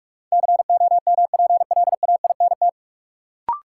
Morse code, heard as a keyed tone, about 46 wpm, 700 Hz. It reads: COMPLAINT
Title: Sentences from Top 1400 Words - New Word Drill Alt - 45wpm